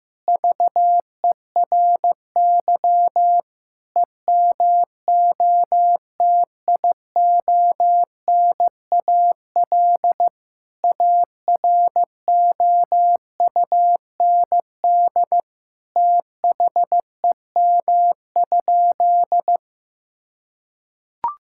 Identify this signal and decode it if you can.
Morse code, heard as a keyed tone, about 15 wpm, 700 Hz. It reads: VERY EMOTIONAL AROUND THEM?
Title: Sentences from Top 1000 Words - Review Part 5 - 15wpm